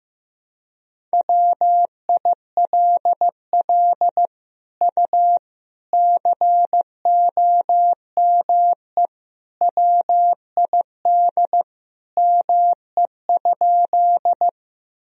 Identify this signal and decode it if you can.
Morse code, heard as a keyed tone, about 16 wpm, 700 Hz. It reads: WILL U COME WID ME?
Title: Sentences from Top 200 Words and Abbreviations 15wpm